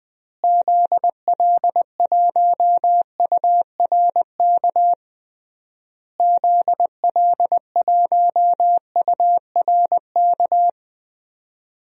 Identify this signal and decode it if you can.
Morse code, heard as a keyed tone, about 20 wpm, 700 Hz. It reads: ZL1URK ZL1URK
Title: DXCC-100 Call Signs - 6 Characters - Repeated 3x - 20wpm